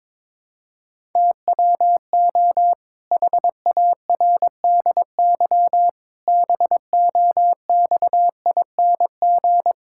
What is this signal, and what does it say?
Morse code, heard as a keyed tone, about 22 wpm, 700 Hz. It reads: TWO HARDY BOXING
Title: Pangram Challenge v2 - 22wpm